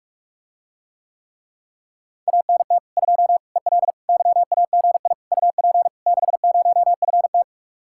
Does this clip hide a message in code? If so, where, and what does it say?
Yes: Morse code from 2.3 to 7.4 s, heard as a keyed tone, about 49 wpm, 700 Hz. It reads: ANT 2 EL YAGI UP 60FT